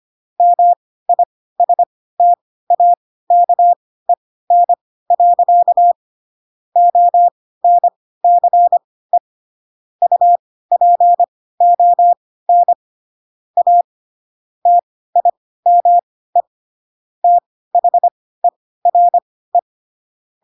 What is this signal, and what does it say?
Morse code, heard as a keyed tone, about 25 wpm, spaced as for 16 wpm, 700 Hz. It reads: MISTAKEN. ONCE UPON A TIME THERE
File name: Pinocchio / Pinocchio ch1 10 25 ik1hge